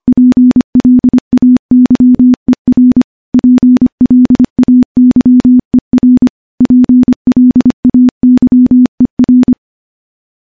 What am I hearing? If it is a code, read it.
Morse code, heard as a keyed tone, about 25 wpm, 250 Hz. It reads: PLAYER PLAYER PLAYER
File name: morse-training-25WPM